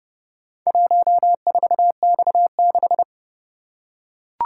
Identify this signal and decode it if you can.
Morse code, heard as a keyed tone, about 28 wpm, 700 Hz. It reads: 14X6